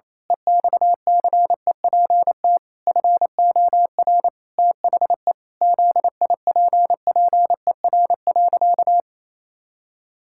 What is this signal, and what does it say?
Morse code, heard as a keyed tone, about 28 wpm, 700 Hz. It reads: EXCEPT FOR THE ZIPPER.
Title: Pangram Challenge v2 - 28wpm